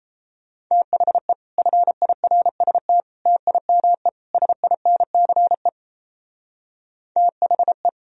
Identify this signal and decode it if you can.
Morse code, heard as a keyed tone, about 34 wpm, 700 Hz. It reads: THE FIRST TIME SINCE THE
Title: Speed-Racing - Sets of 4 Words 28wpm